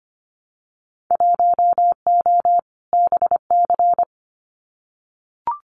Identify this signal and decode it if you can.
Morse code, heard as a keyed tone, about 25 wpm, 700 Hz. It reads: 1O BC
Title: Winter Field Day Exchanges Encore 25wpm